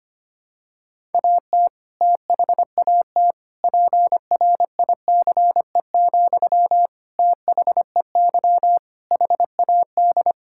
Morse code, heard as a keyed tone, about 25 wpm, 700 Hz. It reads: AT THAT PRICE, THEY HAD
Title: Sentences from Top 1000 Words - Review Part 5 - 25wpm